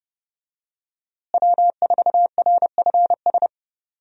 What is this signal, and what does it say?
Morse code, heard as a keyed tone, about 29 wpm, 700 Hz. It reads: W4RFS
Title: USA Call Signs - 5 Characters - Repeated 2x - 30wpm